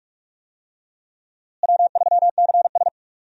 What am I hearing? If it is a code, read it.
Morse code, heard as a keyed tone, about 44 wpm, 700 Hz. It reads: W3CS
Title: USA Call Signs - 4 Characters - Repeated 2x - 45wpm